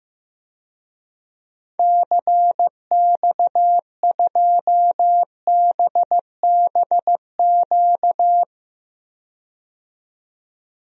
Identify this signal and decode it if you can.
Morse code, heard as a keyed tone, about 15 wpm, 700 Hz. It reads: CX2BBQ
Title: DXCC-100 Call Signs - 6 Characters - 15wpm